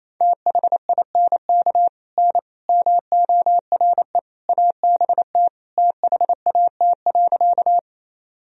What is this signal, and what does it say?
Morse code, heard as a keyed tone, about 28 wpm, 700 Hz. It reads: THINK N MORE ABT THAT.
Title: Sentences from Top 200 Words and Abbreviations 28wpm